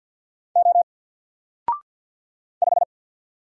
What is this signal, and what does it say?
Morse code, heard as a keyed tone, about 50 wpm, 700 Hz. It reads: O 5
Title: Single Letter-Number Alt 50wpm